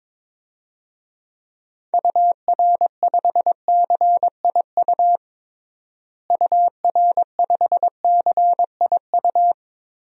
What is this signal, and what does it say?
Morse code, heard as a keyed tone, about 22 wpm, 700 Hz. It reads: UR5CIU UR5CIU
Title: DXCC-100 Call Signs - 6 Characters - Repeated 3x - 22wpm